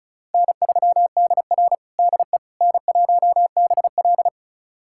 Morse code, heard as a keyed tone, about 34 wpm, 700 Hz. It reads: N3DR DE N1BL
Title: USA Callsign DE Callsign - 4 Characters - 35wpm